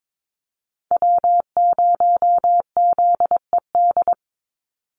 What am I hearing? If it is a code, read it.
Morse code, heard as a keyed tone, about 22 wpm, 700 Hz. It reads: W0ZED
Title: USA Call Signs 22wpm